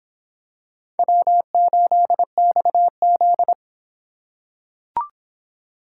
Morse code, heard as a keyed tone, about 25 wpm, 700 Hz. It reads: W8XZ